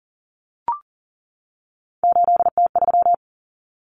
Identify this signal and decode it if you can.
Morse code, heard as a keyed tone, about 40 wpm, 700 Hz. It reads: 8T3
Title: Common Cut Numbers - 3-Digit 40wpm